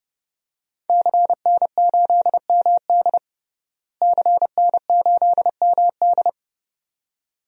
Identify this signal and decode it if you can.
Morse code, heard as a keyed tone, about 30 wpm, 700 Hz. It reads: CN8MD CN8MD
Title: DXCC-100 Call Signs - 5 Characters - Repeated 2x - 30wpm